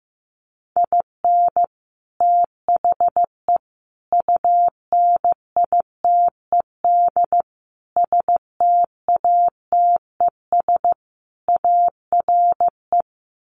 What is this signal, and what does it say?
Morse code, heard as a keyed tone, about 15 wpm, 700 Hz. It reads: IN THE UNITED STATES ARE
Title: Sets of 5 Words Encore 15wpm